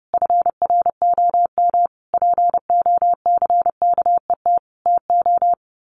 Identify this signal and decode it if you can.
Morse code, heard as a keyed tone, about 30 wpm, 700 Hz. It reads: FROM POCKET TO
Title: A Sound Of Thunder 30wpm